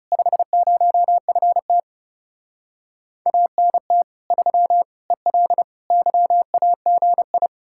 Morse code, heard as a keyed tone, about 37 wpm, 700 Hz. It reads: 50FT ANT 3 EL YAGI
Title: Speed-Racing Common QSO Phrases - 30wpm